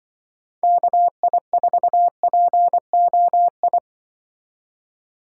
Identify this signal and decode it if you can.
Morse code, heard as a keyed tone, about 23 wpm, 700 Hz. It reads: KI4POI
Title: Speed Racing - USA Call Signs 20wpm